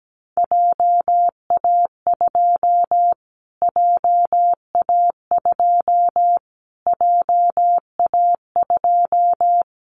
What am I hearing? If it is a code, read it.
Morse code, heard as a keyed tone, about 17 wpm, 700 Hz. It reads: JA2 JA2 JA2
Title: Callsign Prefixes for Top 25 DXCC - Repeated 3x - 17wpm